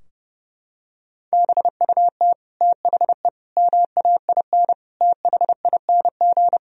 Morse code, heard as a keyed tone, about 30 wpm, 700 Hz. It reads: BUT THE MAIN THING
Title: Sentences from Top 1000 Words - Review Part 2 - 30wpm